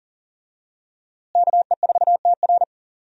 Morse code, heard as a keyed tone, about 40 wpm, 700 Hz. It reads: KE4TR
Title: USA Call Signs - 5 Characters - Repeated 3x - 40wpm